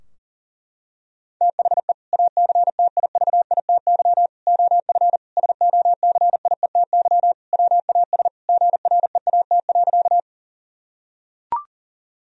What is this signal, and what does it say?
Morse code, heard as a keyed tone, about 40 wpm, 700 Hz. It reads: THE ACTIVITY OF SOCIETY WAS GREAT.
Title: Sentences from Top 1000 Words - Review Part 4 - 40wpm